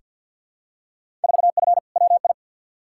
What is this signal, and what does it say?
Morse code, heard as a keyed tone, about 49 wpm, 700 Hz. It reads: 4F WI
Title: Field Day Exchanges - 50wpm